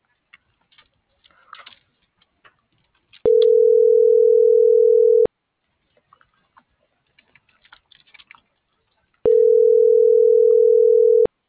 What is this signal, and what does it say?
Ringback tone (North America), 440 + 480 Hz.